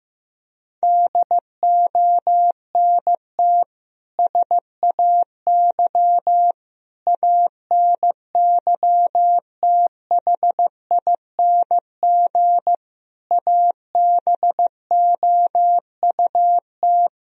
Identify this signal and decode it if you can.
Morse code, heard as a keyed tone, about 15 wpm, 700 Hz. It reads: DONT SAY ANYTHING ABOUT
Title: Sentences from Top 1000 Words - Review Part 1 - 15wpm